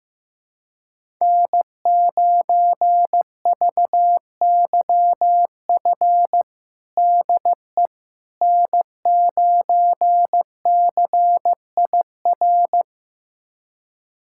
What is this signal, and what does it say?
Morse code, heard as a keyed tone, about 15 wpm, 700 Hz. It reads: N9VYF DE N9CIR
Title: USA Callsign DE Callsign - 5 Characters - 15wpm